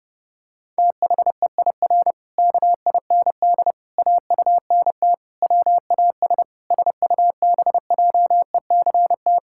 Morse code, heard as a keyed tone, about 30 wpm, 700 Hz. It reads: THEIR KIND AUNT WAS SUBJECT
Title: Pangram Challenge v2 - 30wpm